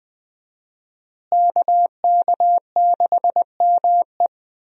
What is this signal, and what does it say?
Morse code, heard as a keyed tone, about 20 wpm, 700 Hz. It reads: KK6ME
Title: USA Call Signs - 5 Characters - Repeated 2x - 20wpm